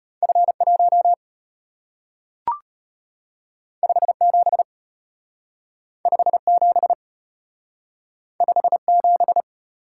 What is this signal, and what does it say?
Morse code, heard as a keyed tone, about 34 wpm, 700 Hz. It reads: F1 57 57 57